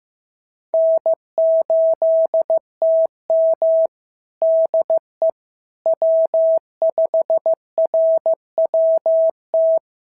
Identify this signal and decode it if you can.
Morse code, heard as a keyed tone, about 15 wpm, 650 Hz. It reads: N8TM DE W5RWT
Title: SOTA QSO Practice 15wpm